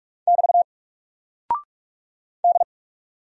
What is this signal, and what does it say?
Morse code, heard as a keyed tone, about 44 wpm, 700 Hz. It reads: <BT> D
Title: Single Character (Letter:Number:Punctuation:Prosign) 45wpm